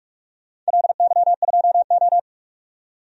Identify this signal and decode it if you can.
Morse code, heard as a keyed tone, about 46 wpm, 700 Hz. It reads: LY2O
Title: DXCC-100 Call Signs - 4 Characters - 45wpm